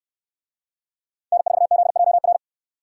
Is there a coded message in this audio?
Yes: Morse code from 1.3 to 2.4 s, heard as a keyed tone, about 63 wpm, 700 Hz. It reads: N4ZJR